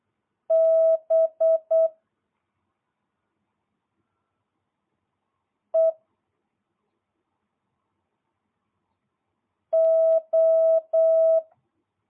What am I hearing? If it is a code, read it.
Morse code, heard as a keyed tone, about 8 wpm, 650 Hz. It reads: B E O